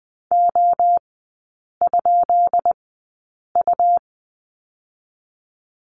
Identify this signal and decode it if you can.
Morse code, heard as a keyed tone, about 20 wpm, spaced as for 8 wpm, 690 Hz. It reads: O?U
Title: 159 Lesson 26e - Letter g - Three Characters 20wpm